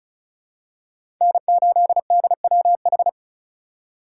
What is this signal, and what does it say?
Morse code, heard as a keyed tone, about 36 wpm, 700 Hz. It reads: N8DWH
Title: USA Call Signs - 5 Characters - 2x Character Spacing - 35wpm